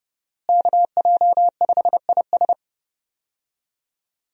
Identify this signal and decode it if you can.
Morse code, heard as a keyed tone, about 30 wpm, 700 Hz. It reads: KJ5IS
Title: USA Call Signs - 5 Characters - Repeated 2x - 30wpm